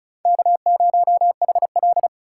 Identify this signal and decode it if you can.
Morse code, heard as a keyed tone, about 36 wpm, 700 Hz. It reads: K0HL
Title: USA Call Signs - 4 Characters - 3x Character Spacing - 35wpm